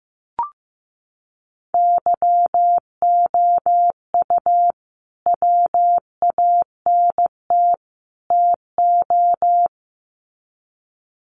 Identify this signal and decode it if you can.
Morse code, heard as a keyed tone, about 15 wpm, 700 Hz. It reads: YOU WANT TO
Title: Sets of 3 Words 15wpm